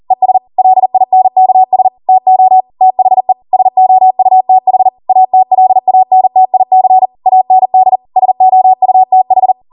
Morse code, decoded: ES LINKS TO THE SOUTH ATLANTIC AND SOUTH